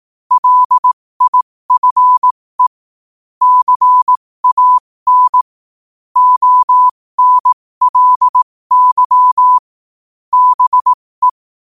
Morse code, heard as a keyed tone, about 18 wpm, spaced as for 15 wpm, 1000 Hz. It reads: LIFE CAN ONLY BE
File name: qotdfiles / 15wpm